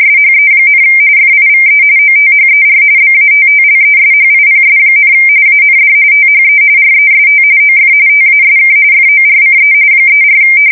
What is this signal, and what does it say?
RTTY, about 45.45 baud. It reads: SHE, FIVE OTHER BATTLESHIPS, AND THE REST OF TASK FORCE 34
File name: fsk170 / USS-New-Jersey